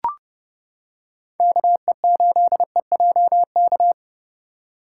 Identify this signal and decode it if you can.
Morse code, heard as a keyed tone, about 31 wpm, 700 Hz. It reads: KE8EJK